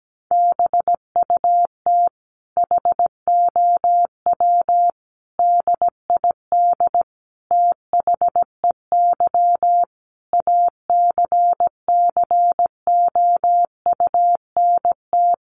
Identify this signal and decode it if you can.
Morse code, heard as a keyed tone, about 17 wpm, 700 Hz. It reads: BUT HOW DID THEY ACCOUNT
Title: Sentences from Top 900 Words 17wpm